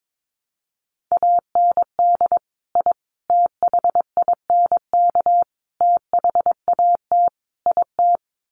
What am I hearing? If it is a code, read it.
Morse code, heard as a keyed tone, about 22 wpm, 700 Hz. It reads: AND I THINK THAT IT
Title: Sets of 5 Words Encore2 22wpm